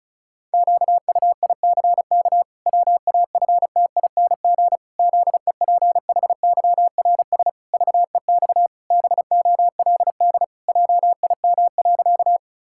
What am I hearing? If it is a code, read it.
Morse code, heard as a keyed tone, about 34 wpm, 700 Hz. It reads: QUICK WAFTING ZEPHYRS VEX BOLD JIM.
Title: Pangram Challenge v2 - 35wpm